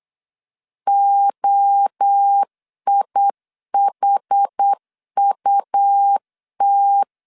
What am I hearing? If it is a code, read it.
Morse code, heard as a keyed tone, about 8 wpm, 800 Hz. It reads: OIHUT
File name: CW / DET / 1 3 1500C S B 8wpm 4141